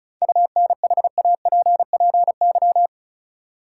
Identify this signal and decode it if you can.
Morse code, heard as a keyed tone, about 36 wpm, 700 Hz. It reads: UNHAPPY
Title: Seven Letter Words - 35wpm